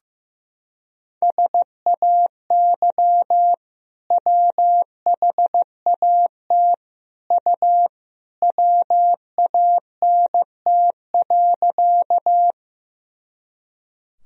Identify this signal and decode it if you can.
Morse code, heard as a keyed tone, about 15 wpm, 700 Hz. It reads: SAY WHAT U WANT.
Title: Sentences from Top 200 Words and Abbreviations 15wpm